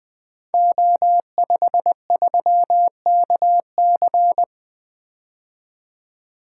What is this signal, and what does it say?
Morse code, heard as a keyed tone, about 20 wpm, 700 Hz. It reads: O53KC
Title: Code Groups - 5 Letter:Numbers - 20wpm